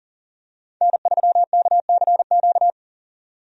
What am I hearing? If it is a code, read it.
Morse code, heard as a keyed tone, about 40 wpm, 700 Hz. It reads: N3KCQ
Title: USA Call Signs - 5 Characters - 40wpm